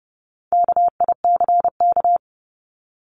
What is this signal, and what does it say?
Morse code, heard as a keyed tone, about 30 wpm, 700 Hz. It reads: KICK
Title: Top 1251-1500 Words 30wpm